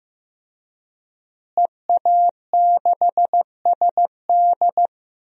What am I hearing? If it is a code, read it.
Morse code, heard as a keyed tone, about 15 wpm, 700 Hz. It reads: EA6SD
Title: DXCC-100 Call Signs - 5 Characters - Repeated 3x - 15wpm